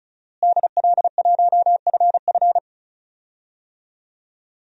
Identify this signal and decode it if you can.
Morse code, heard as a keyed tone, about 34 wpm, 700 Hz. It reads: DL1FF
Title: DXCC-100 Call Signs - 5 Characters - Repeated 2x - 35wpm